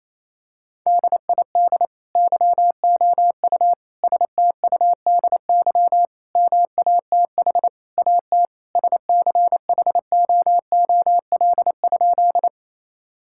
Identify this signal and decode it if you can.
Morse code, heard as a keyed tone, about 28 wpm, 700 Hz. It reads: DID YOU STUDY MATH AT SCHOOL?
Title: Sentences from Top 1000 Words - Review Part 6 - 28wpm